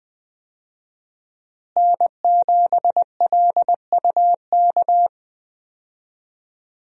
Morse code, heard as a keyed tone, about 20 wpm, 700 Hz. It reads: N7LUK
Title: USA Call Signs - 5 Characters - Repeated 3x - 20wpm